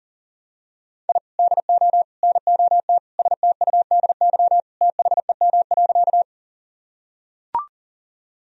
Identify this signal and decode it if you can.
Morse code, heard as a keyed tone, about 37 wpm, 700 Hz. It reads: I DO NOT STUDY THEM.
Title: Sentences from Top 1000 Words - Review Part 1 - 40wpm